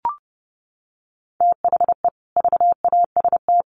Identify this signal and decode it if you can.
Morse code, heard as a keyed tone, about 28 wpm, 700 Hz. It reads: THE VAST
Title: Sets of 5 Words Encore 30wpm